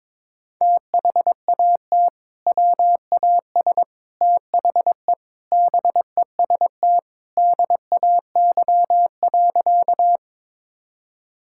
Morse code, heard as a keyed tone, about 22 wpm, 700 Hz. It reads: THAT WAS THE BEST DAY.
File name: practice / 22wpm / Sentences with 3-Word Phrases - 22wpm